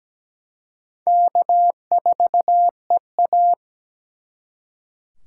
Morse code, heard as a keyed tone, about 20 wpm, 700 Hz. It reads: K4EA
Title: Speed Racing - USA 1x2 Call Signs 17wpm